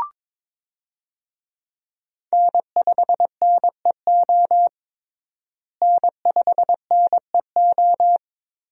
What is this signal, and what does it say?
Morse code, heard as a keyed tone, about 21 wpm, 700 Hz. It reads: N5NEO N5NEO
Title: USA Call Signs - 5 Characters - Repeated 2x - 22wpm